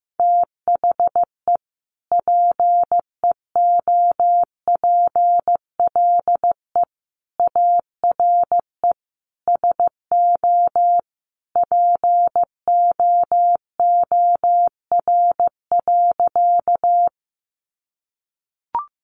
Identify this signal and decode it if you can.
Morse code, heard as a keyed tone, about 15 wpm, 700 Hz. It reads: THE PEOPLE ARE SO POOR.
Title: Sentences from Top 1000 Words 15wpm